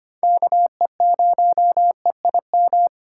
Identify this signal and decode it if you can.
Morse code, heard as a keyed tone, about 25 wpm, 700 Hz. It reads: KE0EIM
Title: USA Call Signs - 6 Characters - 3x Character Spacing - 25wpm